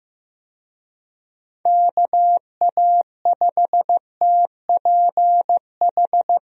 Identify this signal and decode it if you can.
Morse code, heard as a keyed tone, about 15 wpm, 700 Hz. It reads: KA5TPH
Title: USA Call Signs - 6 Characters - 15wpm